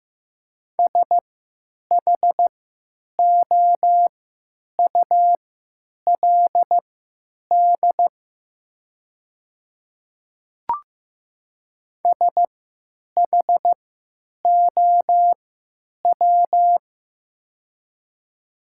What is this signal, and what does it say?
Morse code, heard as a keyed tone, about 15 wpm, spaced as for 9 wpm, 700 Hz. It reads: SHOULD SHOW
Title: 3x Character Spacing - Top 300 Words - 15wpm